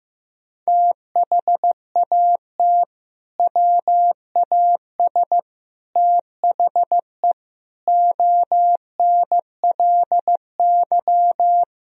Morse code, heard as a keyed tone, about 15 wpm, 700 Hz. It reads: THAT WAS THE ONLY
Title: Sentences from Top 1000 Words - Review Part 1 - 15wpm